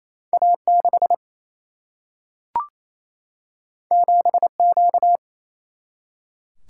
Morse code, heard as a keyed tone, about 28 wpm, 700 Hz. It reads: A6 7Q
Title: Single-Letter Single-Number 28wpm